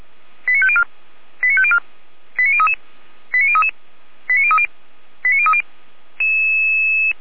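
Five-tone selective call (ZVEI), heard as quick runs of five tones, tones of about 70 ms. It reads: 89504 89504 89033 89033 89030 89030
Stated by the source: ZVEI five-tone selective call received on an SDR, from feuerwehr-orlamuende fire brigade.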